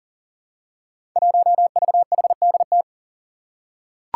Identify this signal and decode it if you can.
Morse code, heard as a keyed tone, about 40 wpm, 700 Hz. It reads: 1VHDT
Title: Five Challenging Letter-Number Combinations 40wpm